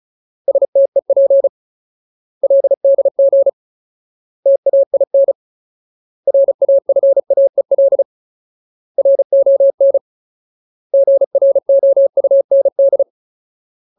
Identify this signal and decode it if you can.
Morse code, heard as a keyed tone, about 34 wpm, 550 Hz. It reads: STEP LDG TAIN RAFAEL RON GROUND
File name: Ultimate Word List / 35WPM Ultimate